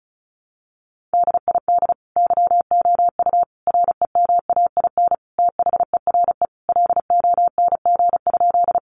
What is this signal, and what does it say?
Morse code, heard as a keyed tone, about 34 wpm, 700 Hz. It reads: DID YOU REMAIN THERE LONG?
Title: Sentences from Top 1200 Words 35wpm